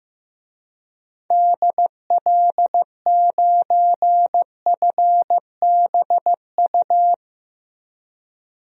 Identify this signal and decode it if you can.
Morse code, heard as a keyed tone, about 15 wpm, 700 Hz. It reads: DL9FBU
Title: DXCC-100 Call Signs - 6 Characters - Repeated 3x - 15wpm